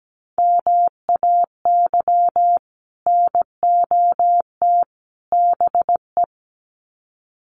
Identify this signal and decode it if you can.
Morse code, heard as a keyed tone, about 17 wpm, 700 Hz. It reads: MAY NOT BE
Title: Sets of 3 Words 17wpm